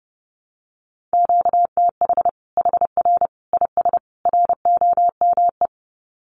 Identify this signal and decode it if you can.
Morse code, heard as a keyed tone, about 30 wpm, 700 Hz. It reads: QTH HR IS ROME
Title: Common QSO Phrases 30wpm